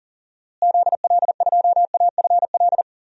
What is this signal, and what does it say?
Morse code, heard as a keyed tone, about 40 wpm, 700 Hz. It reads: ZL2AFL
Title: DXCC-100 Call Signs - 6 Characters - Repeated 2x - 40wpm